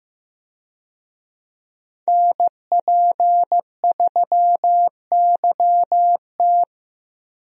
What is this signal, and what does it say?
Morse code, heard as a keyed tone, about 15 wpm, 700 Hz. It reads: NP3YT
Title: USA Call Signs - 5 Characters - Repeated 2x - 15wpm